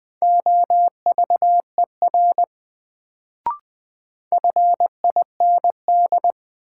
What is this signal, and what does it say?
Morse code, heard as a keyed tone, about 20 wpm, 700 Hz. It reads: OVER FIND
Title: Sets of 2 Words - New Word Drill - 20wpm